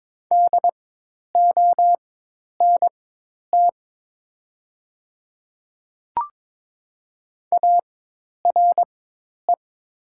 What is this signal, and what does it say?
Morse code, heard as a keyed tone, about 22 wpm, spaced as for 10 wpm, 700 Hz. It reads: DONT ARE
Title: 4x Character Spacing - Top 300 Words - 22wpm